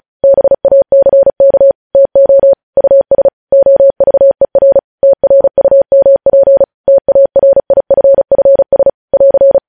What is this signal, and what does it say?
Morse code, heard as a keyed tone, about 36 wpm, 550 Hz. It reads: BACK TO US OVER TRUMP TARIFFS <AR>
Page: news headlines in morse code at 35 wpm